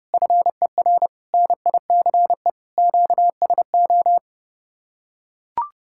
Morse code, heard as a keyed tone, about 29 wpm, 700 Hz. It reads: FER NICE QSO